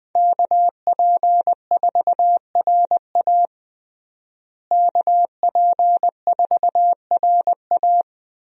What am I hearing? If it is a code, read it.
Morse code, heard as a keyed tone, about 20 wpm, 700 Hz. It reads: KP4RA KP4RA
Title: DXCC-100 Call Signs - 5 Characters - Repeated 3x - 20wpm